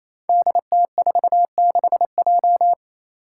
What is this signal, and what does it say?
Morse code, heard as a keyed tone, about 28 wpm, 700 Hz. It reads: DT46J
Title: Five Challenging Letter-Number Combinations 28wpm